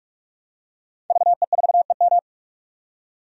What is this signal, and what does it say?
Morse code, heard as a keyed tone, about 45 wpm, 700 Hz. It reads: VE4EM
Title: DXCC-100 Call Signs - 5 Characters - 45wpm